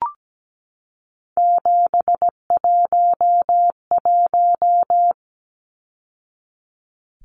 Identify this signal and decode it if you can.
Morse code, heard as a keyed tone, about 17 wpm, 700 Hz. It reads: 711
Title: Numbers - 3-Digit 17wpm